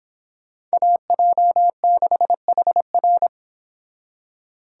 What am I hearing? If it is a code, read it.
Morse code, heard as a keyed tone, about 26 wpm, 700 Hz. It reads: AJ6HR